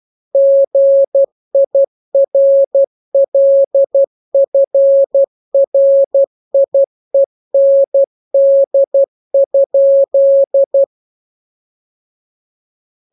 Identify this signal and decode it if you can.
Morse code, heard as a keyed tone, about 12 wpm, 550 Hz. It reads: GIRLFRIEND?
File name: jokes / joke 18 12 0000